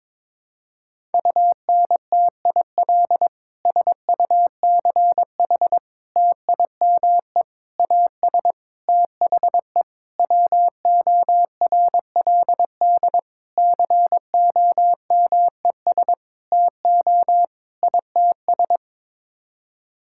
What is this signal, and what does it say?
Morse code, heard as a keyed tone, about 22 wpm, 700 Hz. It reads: UNTIL SUCH TIME AS THE WORLD COMES TO ITS
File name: Sets of 9 Words - 22wpm